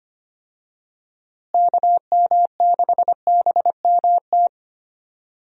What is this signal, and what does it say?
Morse code, heard as a keyed tone, about 25 wpm, 700 Hz. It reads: KM6BMT
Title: USA Call Signs - 6 Characters - Repeated 3x - 25wpm